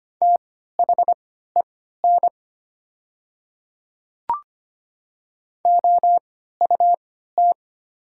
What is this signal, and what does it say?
Morse code, heard as a keyed tone, about 25 wpm, spaced as for 14 wpm, 700 Hz. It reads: THEN OUT
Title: 3x Character Spacing - Top 100 Words - 25wpm